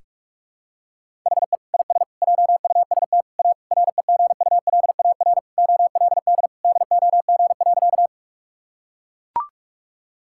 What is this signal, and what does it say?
Morse code, heard as a keyed tone, about 41 wpm, 700 Hz. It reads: HE IS JUST A REGULAR OLD DOG.
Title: Sentences from Top 1000 Words - Review Part 6 - 45wpm